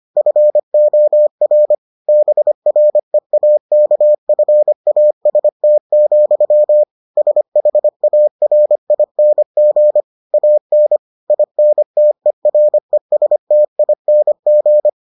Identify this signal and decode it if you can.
Morse code, heard as a keyed tone, about 25 wpm, 600 Hz. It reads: FOR BREAKFAST, SHARING AN INTERESTING